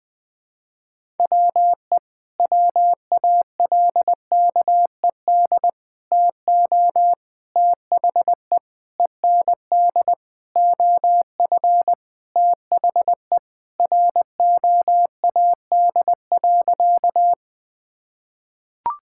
Morse code, heard as a keyed tone, about 20 wpm, 700 Hz. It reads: WE WALKED TO THE END OF THE ROAD.
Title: Sentences with 5-Word Phrases - 20wpm